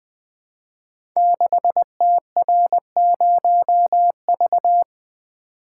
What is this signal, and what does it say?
Morse code, heard as a keyed tone, about 20 wpm, 700 Hz. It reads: 6TR0V
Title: Code Groups - 5 Letter:Numbers - 20wpm